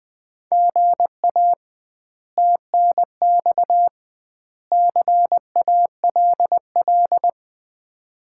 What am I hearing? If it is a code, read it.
Morse code, heard as a keyed tone, about 21 wpm, 700 Hz. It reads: GA TNX CALL
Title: Common QSO Phrases - 2x Word-Spacing - 20wpm